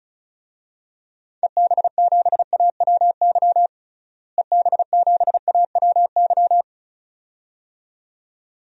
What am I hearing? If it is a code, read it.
Morse code, heard as a keyed tone, about 36 wpm, 700 Hz. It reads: EB7AWY EB7AWY
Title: DXCC-100 Call Signs - 6 Characters - Repeated 2x - 35wpm